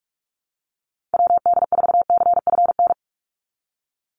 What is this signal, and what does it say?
Morse code, heard as a keyed tone, about 44 wpm, 700 Hz. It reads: WD4CFN
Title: NA SOTA Chaser Callsigns 45wpm